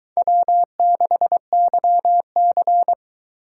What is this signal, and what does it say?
Morse code, heard as a keyed tone, about 22 wpm, 700 Hz. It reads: W6YC